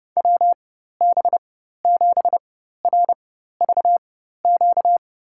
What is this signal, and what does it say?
Morse code, heard as a keyed tone, about 30 wpm, spaced as for 14 wpm, 700 Hz. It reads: WB7RVQ
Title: USA Call Signs - 4x Character Spacing - 30wpm